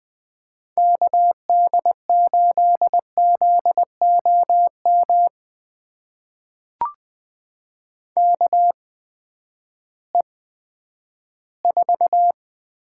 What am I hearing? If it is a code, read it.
Morse code, heard as a keyed tone, about 20 wpm, spaced as for 11 wpm, 700 Hz. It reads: KD8ZOM KE4
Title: USA Call Signs - 6 Characters - 8x Character Spacing - 20wpm